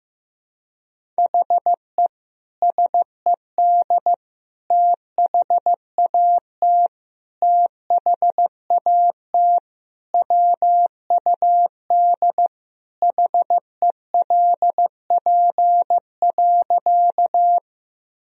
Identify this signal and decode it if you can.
Morse code, heard as a keyed tone, about 15 wpm, 700 Hz. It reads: HE SED THAT THAT WUD HELP.
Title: Sentences from Top 200 Words and Abbreviations 15wpm